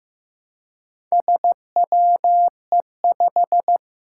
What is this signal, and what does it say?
Morse code, heard as a keyed tone, about 15 wpm, 700 Hz. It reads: SWE5
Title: Four Challenging Letter-Number Combinations 15wpm